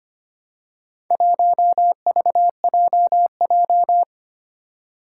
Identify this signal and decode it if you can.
Morse code, heard as a keyed tone, about 25 wpm, 700 Hz. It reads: 1VJJ
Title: Four Challenging Letter-Number Combinations 25wpm